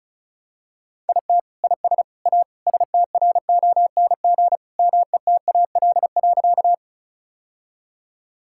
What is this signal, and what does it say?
Morse code, heard as a keyed tone, about 36 wpm, 700 Hz. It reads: IT IS A STRONG METAL.
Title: Sentences from Top 1000 Words - Review Part 6 - 35wpm